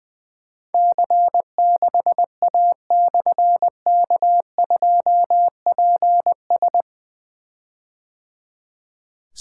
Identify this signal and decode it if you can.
Morse code, heard as a keyed tone, about 21 wpm, 700 Hz. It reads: C6A/K2PS
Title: Difficult Call Signs 20wpm